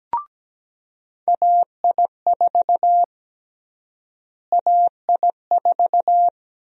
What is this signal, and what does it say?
Morse code, heard as a keyed tone, about 17 wpm, 700 Hz. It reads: AI4 AI4
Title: USA Call Sign Prefixes - Repeated 2x - 17wpm